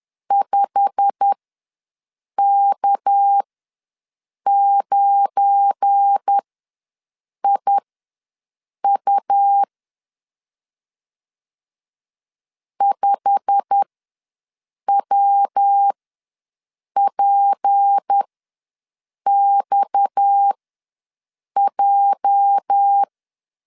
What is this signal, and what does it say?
Morse code, heard as a keyed tone, about 10 wpm, spaced as for 6 wpm, 800 Hz. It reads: 5K9IU 5WPXJ
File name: LEZIONI / 8 YJ45 / 8 YJ45 ws 10wpm 3843